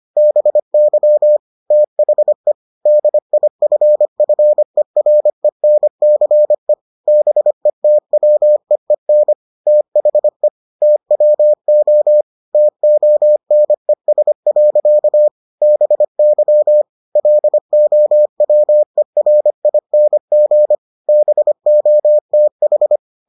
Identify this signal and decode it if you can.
Morse code, heard as a keyed tone, about 25 wpm, 600 Hz. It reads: BY THE DIFFERENCE BETWEEN THE TWO TONES. BY LOWERING BOTH